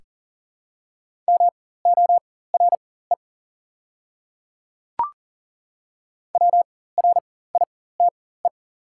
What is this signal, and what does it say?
Morse code, heard as a keyed tone, about 40 wpm, spaced as for 19 wpm, 690 Hz. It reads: MORE WRITE
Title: 4x Character Spacing - Top 100 Words - 40wpm